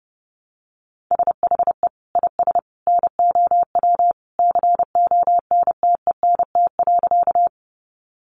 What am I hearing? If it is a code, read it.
Morse code, heard as a keyed tone, about 30 wpm, 700 Hz. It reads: SHE IS NOW CONTENT.
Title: Sentences from Top 900 Words 30wpm